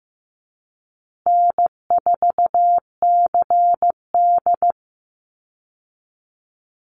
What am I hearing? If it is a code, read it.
Morse code, heard as a keyed tone, about 15 wpm, 700 Hz. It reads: N4CD
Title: Call Signs Repeated 2x - 15wpm